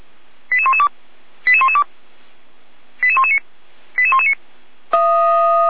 Five-tone selective call (ZVEI), heard as quick runs of five tones, tones of about 70 ms. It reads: 88112 88112 88119 88119
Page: SDR/ZVEI